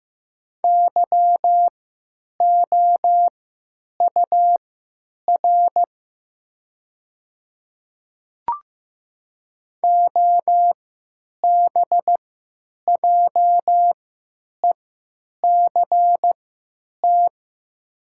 Morse code, heard as a keyed tone, about 15 wpm, spaced as for 9 wpm, 700 Hz. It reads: YOUR OBJECT
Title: 3x Character Spacing - Top 500 Words - 15wpm